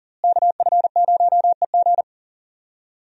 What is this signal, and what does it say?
Morse code, heard as a keyed tone, about 40 wpm, 700 Hz. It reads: KF0EG
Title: USA Call Signs - 5 Characters - Repeated 2x - 40wpm